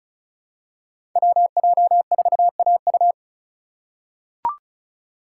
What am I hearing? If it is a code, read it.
Morse code, heard as a keyed tone, about 36 wpm, 700 Hz. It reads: WJ4AU
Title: Code Groups - 5 Letter:Numbers - 35wpm